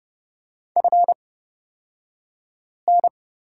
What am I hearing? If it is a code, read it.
Morse code, heard as a keyed tone, about 30 wpm, spaced as for 5 wpm, 700 Hz. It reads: FN
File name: course / 30:10wpm / 235 Lesson 38d - Letter q - Two Characters 30wpm